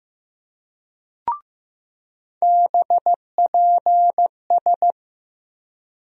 Morse code, heard as a keyed tone, about 15 wpm, 700 Hz. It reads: BPS
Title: USA Call Sign Postfixes - 3 Letters 15wpm